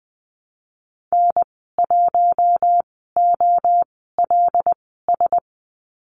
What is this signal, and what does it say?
Morse code, heard as a keyed tone, about 20 wpm, spaced as for 15 wpm, 700 Hz. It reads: N1OLS
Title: 076 Lesson 13g - Letter L - Call Signs 20wpm